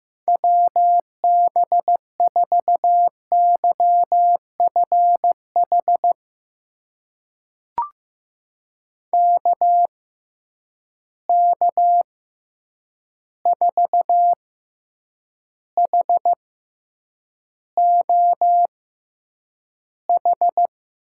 Morse code, heard as a keyed tone, about 15 wpm, spaced as for 8 wpm, 700 Hz. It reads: WB4YFH KK4HOH
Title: USA Call Signs - 6 Characters - 6x Character Spacing - 15wpm